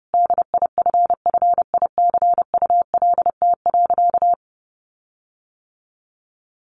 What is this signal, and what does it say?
Morse code, heard as a keyed tone, about 30 wpm, 700 Hz. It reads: DIFFICULT.
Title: Sentences from Top 1300 Words 30wpm